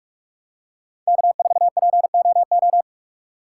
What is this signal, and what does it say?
Morse code, heard as a keyed tone, about 43 wpm, 700 Hz. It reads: K4POO
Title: USA Call Signs - 5 Characters - 3x Character Spacing - 45wpm